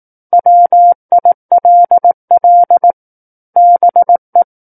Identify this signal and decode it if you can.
Morse code, heard as a keyed tone, about 18 wpm, 700 Hz. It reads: WILL BE
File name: 18 WPM